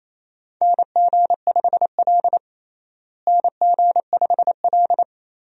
Morse code, heard as a keyed tone, about 28 wpm, 700 Hz. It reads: NG5L NG5L
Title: USA Call Signs - 4 Characters - Repeated 2x - 28wpm